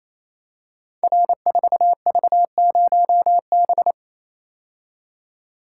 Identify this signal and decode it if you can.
Morse code, heard as a keyed tone, about 28 wpm, 700 Hz. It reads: R4V0B
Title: Code Groups - 5 Letter:Numbers - 28wpm